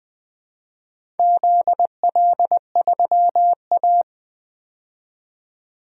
Morse code, heard as a keyed tone, about 20 wpm, 700 Hz. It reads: ZL3A
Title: DXCC-100 Call Signs - 4 Characters - 20wpm